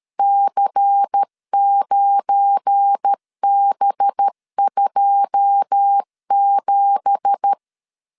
Morse code, heard as a keyed tone, about 13 wpm, 800 Hz. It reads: C9B27